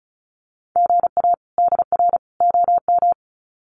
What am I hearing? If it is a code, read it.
Morse code, heard as a keyed tone, about 34 wpm, 700 Hz. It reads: GA DR OM
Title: Common QSO Phrases 35wpm